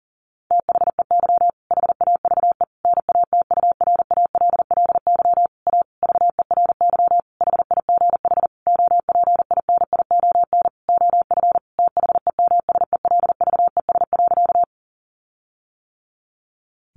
Morse code, heard as a keyed tone, about 40 wpm, 700 Hz. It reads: THEY HAVE NATURALLY A VERY HIGH OPINION OF THEMSELVES.
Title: Sentences from Top 1300 Words 40wpm